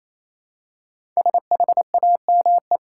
Morse code, heard as a keyed tone, about 28 wpm, 700 Hz. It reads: SHAME
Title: Five Letter Words - 28wpm